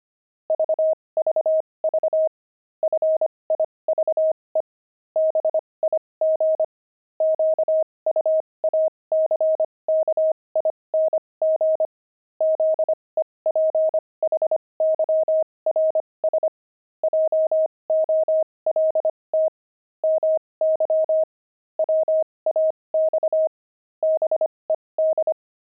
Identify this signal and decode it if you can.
Morse code, heard as a keyed tone, about 25 wpm, spaced as for 20 wpm, 630 Hz. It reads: VVV FIVE BIG QUACKING ZEPHYRS JOLT MY WAX BED
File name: pangram-fs-20-cs-25-five